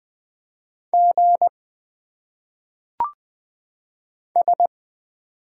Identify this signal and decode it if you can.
Morse code, heard as a keyed tone, about 20 wpm, 700 Hz. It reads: G S
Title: Single Letters - Phonetic Pronunciation 20wpm